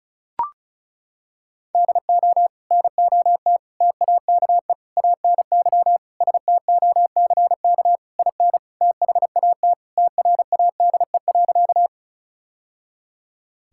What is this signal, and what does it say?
Morse code, heard as a keyed tone, about 36 wpm, 700 Hz. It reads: DO NOT TAKE ANY STOCK IN THAT TRADE.
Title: Sentences from Top 1000 Words - Review Part 5 - 35wpm